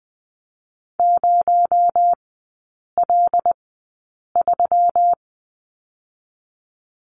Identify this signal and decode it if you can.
Morse code, heard as a keyed tone, about 20 wpm, 700 Hz. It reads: 0 L 3
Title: 218 Lesson 35e - Number 0 - Three Characters 20wpm